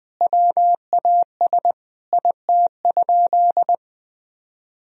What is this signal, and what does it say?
Morse code, heard as a keyed tone, about 20 wpm, 700 Hz. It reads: WAS IT?
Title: Sentences from Top 1000 Words - Review Part 4 - 20wpm